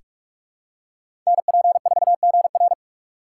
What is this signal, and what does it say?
Morse code, heard as a keyed tone, about 46 wpm, 700 Hz. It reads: NP4GR
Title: USA Call Signs - 3x Character Spacing - 45wpm